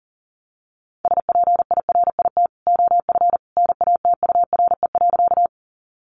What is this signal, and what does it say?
Morse code, heard as a keyed tone, about 40 wpm, 700 Hz. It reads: SPIRIT OF NATURE.
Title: Sentences from Top 1400 Words 40wpm